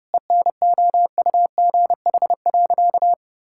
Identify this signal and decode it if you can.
Morse code, heard as a keyed tone, about 30 wpm, 700 Hz. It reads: ENOUGH.
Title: Sentences from Top 1000 Words - Review Part 5 - 30wpm